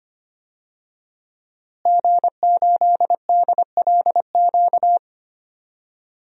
Morse code, heard as a keyed tone, about 25 wpm, 700 Hz. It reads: G8DLQ
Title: DXCC-100 Call Signs - 5 Characters - Repeated 2x - 25wpm